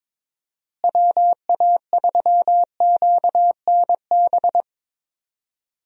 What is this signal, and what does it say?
Morse code, heard as a keyed tone, about 22 wpm, 700 Hz. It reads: WA3QNB
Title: USA Call Signs - 6 Characters - Repeated 2x - 22wpm